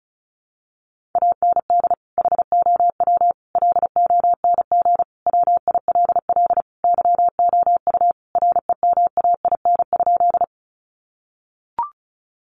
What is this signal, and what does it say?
Morse code, heard as a keyed tone, about 34 wpm, 700 Hz. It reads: AND HOW LONG WILL YOU REMAIN?
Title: Sentences from Top 1200 Words 35wpm